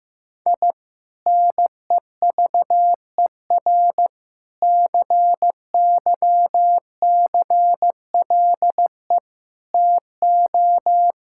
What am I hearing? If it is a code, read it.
Morse code, heard as a keyed tone, about 15 wpm, 700 Hz. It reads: I NEVER CYCLE TO
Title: Sentences from Top 1000 Words - Review Part 6 - 15wpm